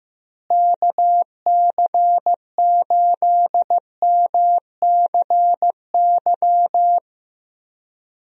Morse code, heard as a keyed tone, about 15 wpm, 700 Hz. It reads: KC8MCY
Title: USA Call Signs - 6 Characters - Repeated 3x - 15wpm